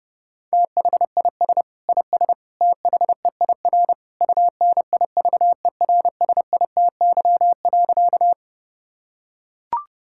Morse code, heard as a keyed tone, about 31 wpm, 700 Hz. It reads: THIS IS THEIR UNIVERSITY.
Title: Sentences from Top 1000 Words - Review Part 5 - 30wpm